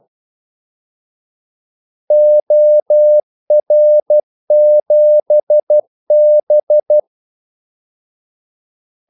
Morse code, heard as a keyed tone, about 12 wpm, 600 Hz. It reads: OR7B
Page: 2 ekstra ordmellemrum mellem kaldesignalerne